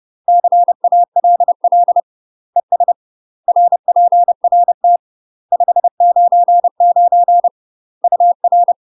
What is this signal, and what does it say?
Morse code, heard as a keyed tone, about 30 wpm, 700 Hz. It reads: CALL ES RPRT 599 UR
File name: UK MASTER QSOs TRAINING / 30WPM / UK MASTER QSOs 0019